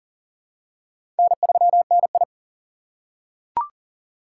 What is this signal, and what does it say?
Morse code, heard as a keyed tone, about 40 wpm, 700 Hz. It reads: N3NI
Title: USA Call Signs - 6x Character Spacing - 40wpm